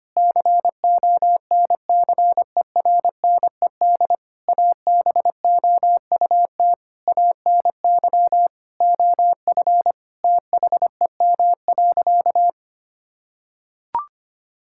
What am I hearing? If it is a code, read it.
Morse code, heard as a keyed tone, about 25 wpm, 700 Hz. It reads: CONCERNED ABOUT ANY OF THEM.
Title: Sentences from Top 1000 Words - Review Part 6 - 25wpm